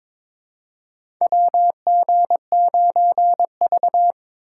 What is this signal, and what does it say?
Morse code, heard as a keyed tone, about 22 wpm, 700 Hz. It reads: WG9V
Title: USA Call Signs - 3x Character Spacing - 22wpm